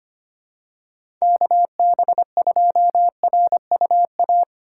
Morse code, heard as a keyed tone, about 25 wpm, 690 Hz. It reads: KB2RUA
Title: USA Call Signs - 6 Characters - 3x Character Spacing - 25wpm